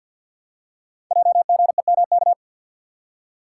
Morse code, heard as a keyed tone, about 52 wpm, 700 Hz. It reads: JGEMK
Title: Code Groups - 5 Letters - 50wpm